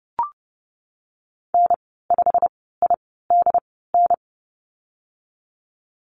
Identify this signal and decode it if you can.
Morse code, heard as a keyed tone, about 30 wpm, 700 Hz. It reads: N 5 I D N
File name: Lesson 19g - Letter w - Call Signs 30wpm